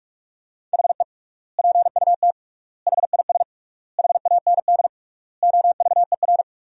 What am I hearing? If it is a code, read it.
Morse code, heard as a keyed tone, about 44 wpm, 700 Hz. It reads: HE PUT HIS HAND OVER
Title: 3x Word-Spacing - Sentences from Top 400 Words - 45wpm